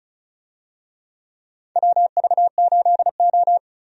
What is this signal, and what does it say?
Morse code, heard as a keyed tone, about 34 wpm, 700 Hz. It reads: WV8O
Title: USA Call Signs - 4 Characters - 35wpm